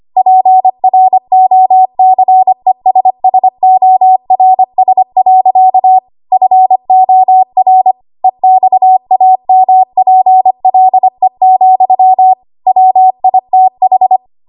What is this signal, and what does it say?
Morse code, heard as a keyed tone, about 25 wpm, 750 Hz. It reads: PROCESSORS. FOR EXAMPLE, WITH
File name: Archive / 25 WPM / 25WPM